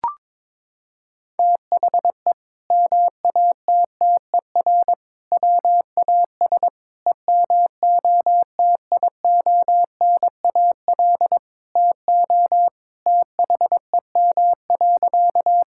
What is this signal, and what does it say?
Morse code, heard as a keyed tone, about 22 wpm, 700 Hz. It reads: THE MATTER WAS EMOTIONAL TO THEM.
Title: Sentences from Top 1000 Words - Review Part 5 - 22wpm